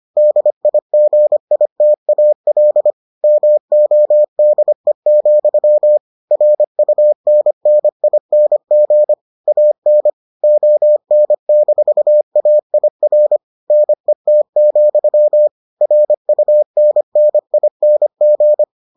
Morse code, heard as a keyed tone, about 25 wpm, 600 Hz. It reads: DIGITAL MODE, RUNNING AN ON-AIR NET, RUNNING